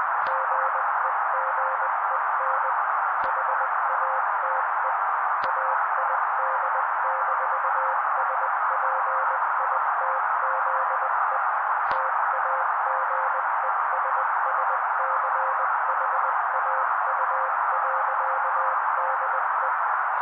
Morse code, heard as a keyed tone, about 20 wpm, 530 Hz. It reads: GEGEN HATE AID-SPITZE TAGESSCHAU.DE
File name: daily / 18wpm noise